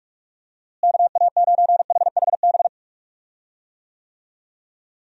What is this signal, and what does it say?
Morse code, heard as a keyed tone, about 42 wpm, 700 Hz. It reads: KA9HHB
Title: USA Call Signs - 6 Characters - Repeated 3x - 45wpm